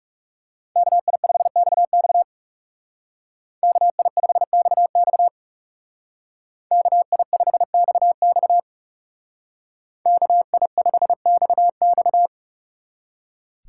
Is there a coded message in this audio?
Yes: Morse code from 0.8 to 12.3 s, heard as a keyed tone, about 39 wpm, 700 Hz. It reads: KI5XX KI5XX KI5XX KI5XX